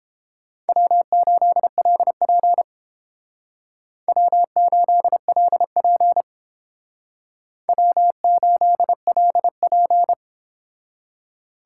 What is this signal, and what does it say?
Morse code, heard as a keyed tone, about 30 wpm, 700 Hz. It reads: W8LP W8LP W8LP